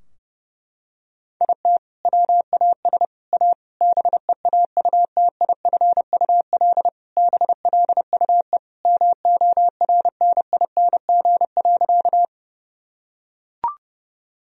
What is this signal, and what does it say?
Morse code, heard as a keyed tone, about 30 wpm, 700 Hz. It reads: IT WAS A BEAUTIFUL BLUE MORNING.
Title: Sentences from Top 1000 Words - Review Part 3 - 30wpm